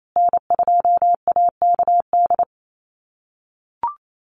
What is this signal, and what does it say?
Morse code, heard as a keyed tone, about 28 wpm, 700 Hz. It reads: N2AKD
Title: USA Call Signs Repeated 3x 28wpm